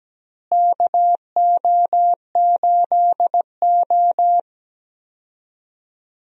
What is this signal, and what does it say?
Morse code, heard as a keyed tone, about 17 wpm, 700 Hz. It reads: KO8O
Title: USA Call Signs - 4 Characters - Repeated 2x - 17wpm